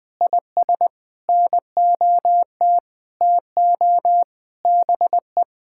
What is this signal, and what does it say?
Morse code, heard as a keyed tone, about 21 wpm, 700 Hz. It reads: IS NOT TO BE
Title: Sentences with 6-Word Phrases - 20wpm